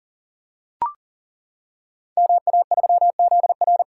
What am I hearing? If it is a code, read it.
Morse code, heard as a keyed tone, about 41 wpm, 700 Hz. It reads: MA3ZR
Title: Code Groups - 5 Letter:Numbers - 40wpm